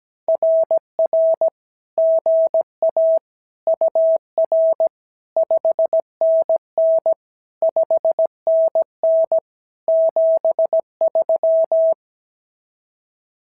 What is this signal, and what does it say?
Morse code, heard as a keyed tone, about 17 wpm, 650 Hz. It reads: RR GA UR 5NN 5NN 73
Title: SOTA QSO Practice 17wpm